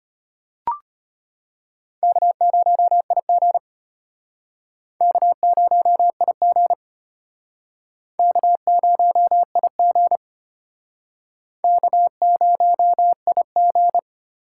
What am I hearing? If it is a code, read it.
Morse code, heard as a keyed tone, about 34 wpm, 700 Hz. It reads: K0IG K0IG K0IG K0IG